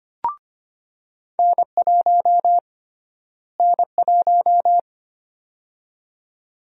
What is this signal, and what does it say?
Morse code, heard as a keyed tone, about 25 wpm, 700 Hz. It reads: N1 N1
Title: USA Call Sign Prefixes - Repeated 2x - 25wpm